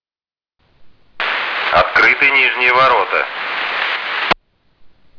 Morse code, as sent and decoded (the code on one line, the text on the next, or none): none